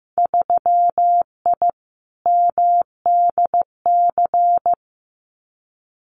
Morse code, heard as a keyed tone, about 15 wpm, 700 Hz. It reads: 3I MDC
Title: Winter Field Day Exchanges Encore 15wpm